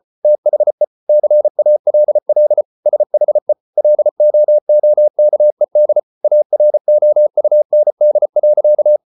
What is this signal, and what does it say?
Morse code, heard as a keyed tone, about 34 wpm, 600 Hz. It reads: THE CALL SHE LOOKED AROUND.